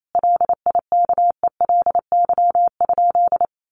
Morse code, heard as a keyed tone, about 28 wpm, 700 Hz. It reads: LIKELY?
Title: Sentences from Top 1100 Words 28wpm